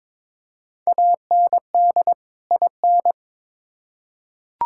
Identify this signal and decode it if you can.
Morse code, heard as a keyed tone, about 21 wpm, 700 Hz. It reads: AND IN
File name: Build A Phrase - 2-Word Phrases - 22wpm